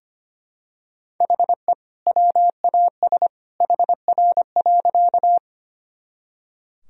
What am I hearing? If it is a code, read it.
Morse code, heard as a keyed tone, about 25 wpm, 700 Hz. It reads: HE WAS HR.
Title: Sentences from Top 200 Words and Abbreviations 25wpm